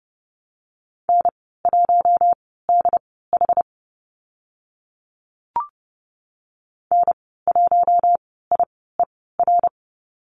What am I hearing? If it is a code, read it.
Morse code, heard as a keyed tone, about 30 wpm, spaced as for 17 wpm, 700 Hz. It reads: N1DH N1IER
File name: Lesson 14g - Number 2 - Call Signs 30wpm